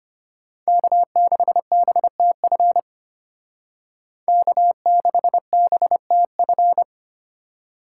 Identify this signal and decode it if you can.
Morse code, heard as a keyed tone, about 33 wpm, 700 Hz. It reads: K6BTF K6BTF